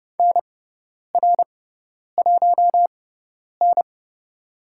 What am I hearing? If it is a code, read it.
Morse code, heard as a keyed tone, about 30 wpm, spaced as for 10 wpm, 700 Hz. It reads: NR1N
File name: Lesson 17g - Letter u - Call Signs 30wpm